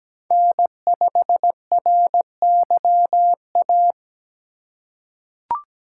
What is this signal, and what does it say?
Morse code, heard as a keyed tone, about 17 wpm, 700 Hz. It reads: N5RYA
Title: USA Call Signs - 3x Character Spacing - 17wpm